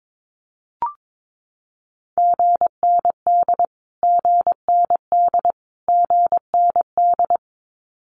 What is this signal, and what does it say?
Morse code, heard as a keyed tone, about 22 wpm, 700 Hz. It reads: GND GND GND
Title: Abbreviations Repeated 3x - 22wpm